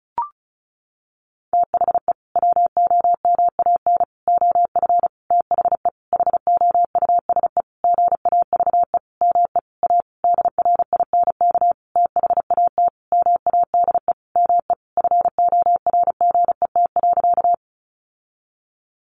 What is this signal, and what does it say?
Morse code, heard as a keyed tone, about 34 wpm, 700 Hz. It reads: THE WOMAN OF THE HOUSE GAVE ME A DRINK THAT MADE ME FORGET.
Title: Sentences from Top 1500 Words 35wpm